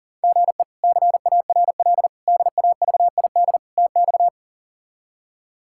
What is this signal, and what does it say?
Morse code, heard as a keyed tone, about 40 wpm, 690 Hz. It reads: GE CARL DAVID TX
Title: K1USN SST QSO Practice - 40wpm